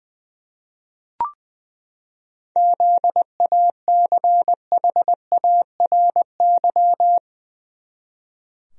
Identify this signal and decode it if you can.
Morse code, heard as a keyed tone, about 20 wpm, 700 Hz. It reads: ZACHARY
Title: USA Common Names - 20wpm